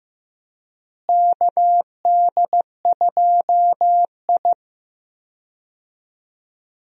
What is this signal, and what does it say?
Morse code, heard as a keyed tone, about 15 wpm, 700 Hz. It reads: KD2I
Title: USA Call Signs - 4 Characters - Repeated 2x - 15wpm